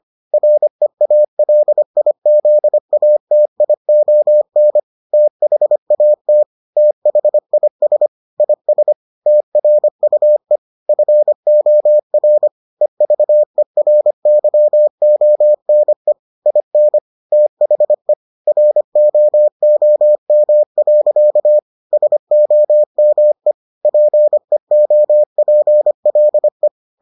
Morse code, heard as a keyed tone, about 25 wpm, 600 Hz. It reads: REALIZATION THAT THIS IS TRUE FOR EVERYONE IN THE ROOM. SOME PEOPLE